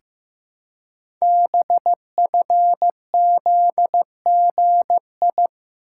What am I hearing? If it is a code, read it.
Morse code, heard as a keyed tone, about 15 wpm, 700 Hz. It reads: BFZGI
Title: Code Groups - 5 Letters - 15wpm